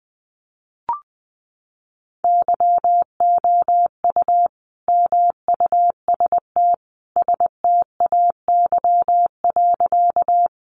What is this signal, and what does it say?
Morse code, heard as a keyed tone, about 20 wpm, 700 Hz. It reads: YOU MUST STAY.
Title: Sentences from Top 500 Words 20wpm